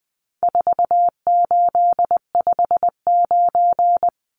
Morse code, heard as a keyed tone, about 20 wpm, 700 Hz. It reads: 4859
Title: Numbers - 4-Digit 20wpm